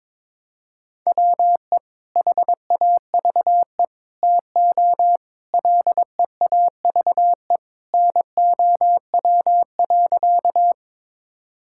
Morse code, heard as a keyed tone, about 22 wpm, 700 Hz. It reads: WE HAVE TO LEAVE NOW.
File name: practice / 22wpm / Sentences with 3-Word Phrases - 22wpm